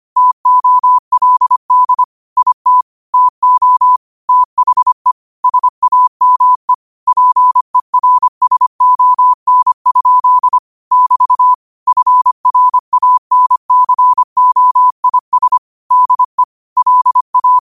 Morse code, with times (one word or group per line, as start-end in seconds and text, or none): 0.2-2.0 TOLD
2.4-2.8 IT
3.1-4.0 TO
4.3-5.1 THE
5.4-6.7 SAME
7.1-10.6 PERSON?
10.9-11.5 <BT>
11.9-15.6 FRANCOIS
15.9-16.4 DE
16.8-17.6 LA